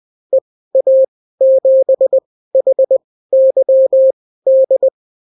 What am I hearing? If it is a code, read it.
Morse code, heard as a keyed tone, about 20 wpm, spaced as for 14 wpm, 530 Hz. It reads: EA7HYD
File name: CW QSO libroCW4 20-10ppm